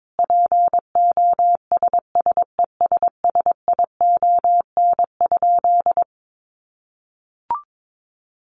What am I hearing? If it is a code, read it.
Morse code, heard as a keyed tone, about 22 wpm, 700 Hz. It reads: POSSESSION?
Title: Sentences from Top 1400 Words 22wpm